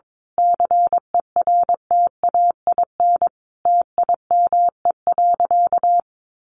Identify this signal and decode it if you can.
Morse code, heard as a keyed tone, about 22 wpm, 700 Hz. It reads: CERTAIN TIME.
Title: Sentences from Top 1000 Words 22wpm